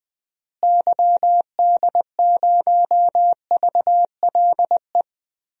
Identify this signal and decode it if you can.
Morse code, heard as a keyed tone, about 20 wpm, 700 Hz. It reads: YD0VLE
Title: DXCC-100 Call Signs - 6 Characters - Repeated 3x - 20wpm